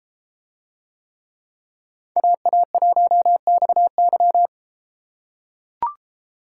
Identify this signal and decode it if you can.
Morse code, heard as a keyed tone, about 32 wpm, 700 Hz. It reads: AA1XY